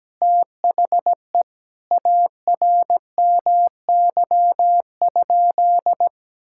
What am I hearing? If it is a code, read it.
Morse code, heard as a keyed tone, about 17 wpm, 700 Hz. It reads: THE ARMY?
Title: Sentences from Top 1000 Words - Review Part 6 - 17wpm